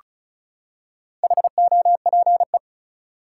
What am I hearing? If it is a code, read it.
Morse code, heard as a keyed tone, about 36 wpm, 700 Hz. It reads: HOPE
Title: Four Letter Words - 35wpm